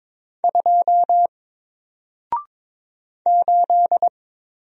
Morse code, heard as a keyed tone, about 22 wpm, 700 Hz. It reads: 2 8
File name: Single-Letter-Number-22wpm